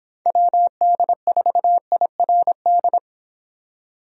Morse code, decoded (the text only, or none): WD4IRD